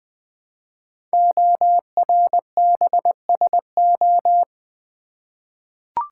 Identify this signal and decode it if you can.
Morse code, heard as a keyed tone, about 20 wpm, 700 Hz. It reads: ORBSO
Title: Code Groups - 5 Letter:Numbers - 20wpm